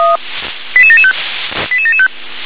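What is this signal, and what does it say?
Five-tone selective call (ZVEI), heard as quick runs of five tones, tones of about 70 ms. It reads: 88775 88775
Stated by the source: ZVEI five-tone call received by SDR.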